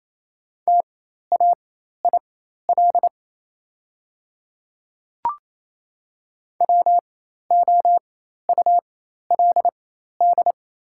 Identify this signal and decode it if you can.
Morse code, heard as a keyed tone, about 28 wpm, spaced as for 13 wpm, 690 Hz. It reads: TAIL WOULD
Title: 4x Character Spacing - Top 500 Words - 28wpm